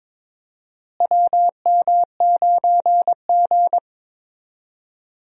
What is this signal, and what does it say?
Morse code, heard as a keyed tone, about 22 wpm, 700 Hz. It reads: WM9G
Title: USA Call Signs - 4 Characters - Repeated 3x - 22wpm